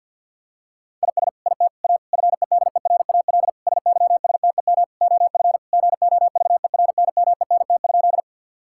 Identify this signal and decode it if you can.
Morse code, heard as a keyed tone, about 51 wpm, 700 Hz. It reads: IS IT A FEDERAL SYSTEM OF GOVERNMENT?
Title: Sentences from Top 1000 Words - Review Part 5 - 50wpm